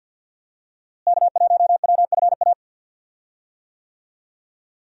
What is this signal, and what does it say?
Morse code, heard as a keyed tone, about 50 wpm, 700 Hz. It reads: K1WFA
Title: USA Call Signs - 5 Characters - Repeated 2x - 50wpm